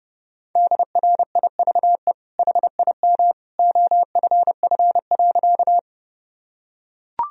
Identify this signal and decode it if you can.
Morse code, heard as a keyed tone, about 32 wpm, 700 Hz. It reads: DRIVE HIM OFF.
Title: Sentences from Top 1000 Words - Review Part 2 - 30wpm